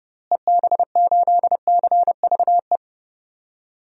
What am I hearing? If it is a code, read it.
Morse code, heard as a keyed tone, about 30 wpm, 700 Hz. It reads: EB8CVE
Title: DXCC-100 Call Signs - 6 Characters - Repeated 3x - 30wpm